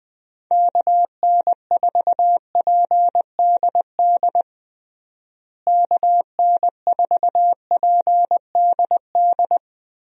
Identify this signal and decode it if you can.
Morse code, heard as a keyed tone, about 20 wpm, 700 Hz. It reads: KN4PDD KN4PDD
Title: USA Call Signs - 6 Characters - Repeated 2x - 20wpm